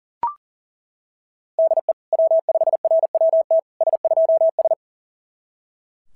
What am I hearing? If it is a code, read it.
Morse code, heard as a keyed tone, about 35 wpm, 660 Hz. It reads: DE W5RWT S2S
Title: SOTA QSO Practice 40wpm